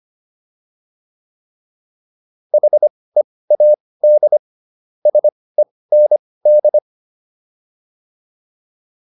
Morse code, heard as a keyed tone, about 25 wpm, spaced as for 18 wpm, 600 Hz. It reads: HEAD SEND
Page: intermediate-practice-files